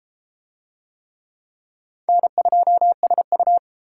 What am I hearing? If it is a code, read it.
Morse code, heard as a keyed tone, about 27 wpm, 700 Hz. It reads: N2SU
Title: Speed Racing - USA 1x2 Call Signs 22wpm